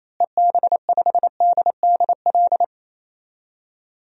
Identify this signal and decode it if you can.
Morse code, heard as a keyed tone, about 28 wpm, 700 Hz. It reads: EB5DDL
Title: DXCC-100 Call Signs - 6 Characters - Repeated 2x - 28wpm